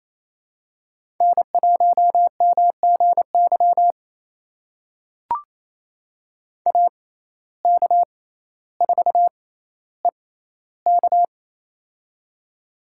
Morse code, heard as a keyed tone, about 28 wpm, spaced as for 14 wpm, 700 Hz. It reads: N1MGY AK4EK
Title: USA Call Signs - 5 Characters - 6x Character Spacing - 28wpm